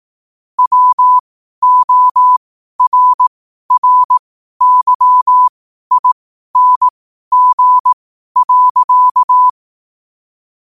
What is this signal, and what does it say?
Morse code, heard as a keyed tone, about 18 wpm, spaced as for 12 wpm, 1000 Hz. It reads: WORRYING.
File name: qotdfiles / 12wpm